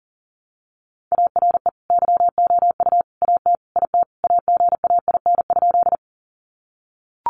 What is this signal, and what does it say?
Morse code, heard as a keyed tone, about 40 wpm, 700 Hz. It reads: ARE YOU AT IT AGAIN?
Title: Sentences from Top 200 Words 40wpm